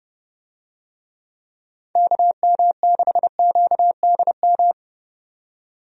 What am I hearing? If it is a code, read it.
Morse code, heard as a keyed tone, about 30 wpm, 700 Hz. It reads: KM6QDM
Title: USA Call Signs - 6 Characters - 30wpm